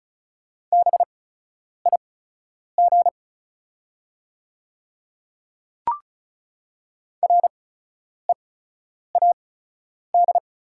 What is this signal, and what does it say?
Morse code, heard as a keyed tone, about 36 wpm, spaced as for 10 wpm, 690 Hz. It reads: BIG READ
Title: 8x Character Spacing - Top 300 Words - 35wpm